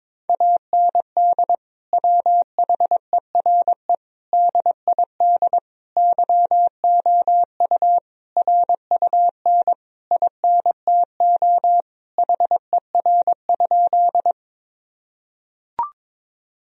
Morse code, heard as a keyed tone, about 22 wpm, 700 Hz. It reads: AND WHERE DID YOU RUN INTO HER?
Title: Sentences from Top 1000 Words - Review Part 1 - 22wpm